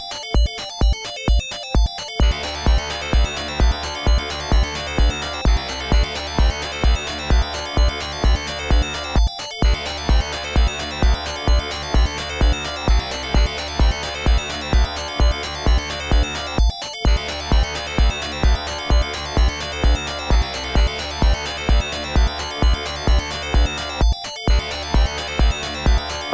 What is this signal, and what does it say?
There is no code.